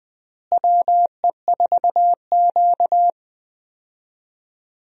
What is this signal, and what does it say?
Morse code, heard as a keyed tone, about 20 wpm, 700 Hz. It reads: WE4Q
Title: USA Call Signs - 4 Characters - Repeated 3x - 20wpm